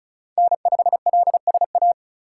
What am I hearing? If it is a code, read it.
Morse code, heard as a keyed tone, about 34 wpm, 700 Hz. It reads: N5LSA
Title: USA Call Signs - 2x Character Spacing - 35wpm